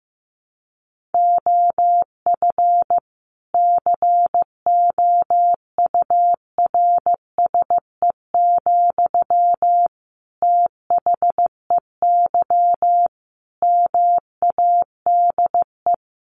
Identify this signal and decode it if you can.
Morse code, heard as a keyed tone, about 15 wpm, 700 Hz. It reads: OF COURSE, THEY MADE
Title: Sentences from Top 1400 Words 15wpm